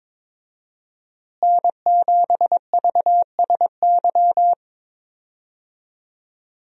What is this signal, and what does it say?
Morse code, heard as a keyed tone, about 22 wpm, 700 Hz. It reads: N7VSY
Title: USA Call Signs - 5 Characters - Repeated 3x - 22wpm